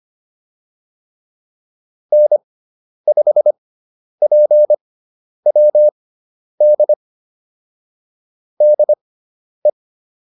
Morse code, heard as a keyed tone, about 25 wpm, spaced as for 10 wpm, 600 Hz. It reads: N5PWD DE
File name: QSO208 10